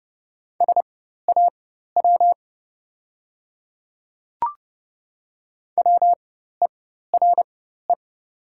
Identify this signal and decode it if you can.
Morse code, heard as a keyed tone, about 30 wpm, spaced as for 14 wpm, 700 Hz. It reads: SAW WERE
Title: 4x Character Spacing - Top 300 Words - 30wpm